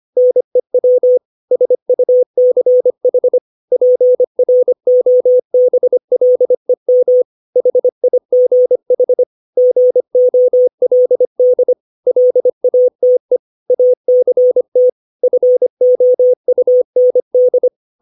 Morse code, heard as a keyed tone, about 25 wpm, 500 Hz. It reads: NEW SUCH PROBLEM HIGH GOLD LATE ACT FOUND
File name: 632 words / 632 Words 25WPM